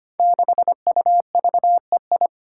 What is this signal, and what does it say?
Morse code, heard as a keyed tone, about 25 wpm, 700 Hz. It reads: 6UVEI
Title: Five Challenging Letter-Number Combinations 25wpm